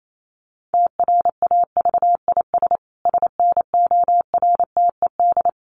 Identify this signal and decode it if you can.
Morse code, heard as a keyed tone, about 28 wpm, 700 Hz. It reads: TRAVIS SNORTED
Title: A Sound Of Thunder 28wpm